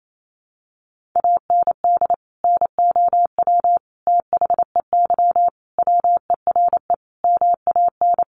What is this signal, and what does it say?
Morse code, heard as a keyed tone, about 28 wpm, 700 Hz. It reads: AND NOW THEY WERE MAN
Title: Sentences from Top 1100 Words 28wpm